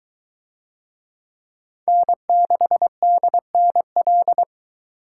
Morse code, heard as a keyed tone, about 22 wpm, 700 Hz. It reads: N6DNL